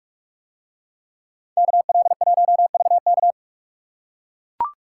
Morse code, heard as a keyed tone, about 44 wpm, 700 Hz. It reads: KL1VK